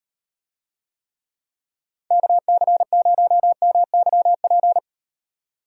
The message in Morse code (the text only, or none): KC0MYP